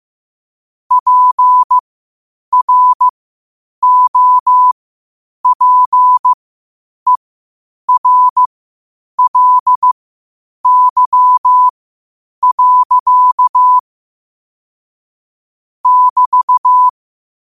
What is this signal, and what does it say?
Morse code, heard as a keyed tone, about 15 wpm, spaced as for 8 wpm, 1000 Hz. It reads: PROPERLY. <BT>
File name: qotdfiles / 8wpm